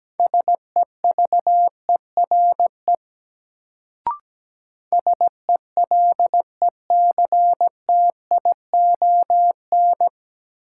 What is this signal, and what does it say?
Morse code, heard as a keyed tone, about 17 wpm, 700 Hz. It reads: SEVERE SELECTION
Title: Sentences from Top 1400 Words - New Word Drill - 17wpm